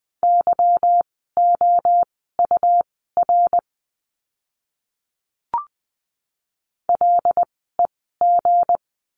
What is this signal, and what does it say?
Morse code, heard as a keyed tone, about 20 wpm, spaced as for 15 wpm, 700 Hz. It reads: YOUR LEG
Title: 160 Lesson 26f - Letter g - Words 20wpm